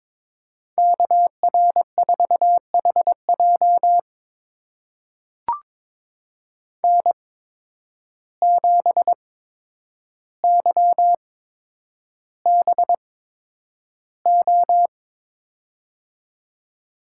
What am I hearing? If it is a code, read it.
Morse code, heard as a keyed tone, about 22 wpm, spaced as for 9 wpm, 700 Hz. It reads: KR4HJ N7YBO
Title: USA Call Signs - 5 Characters - 8x Character Spacing - 22wpm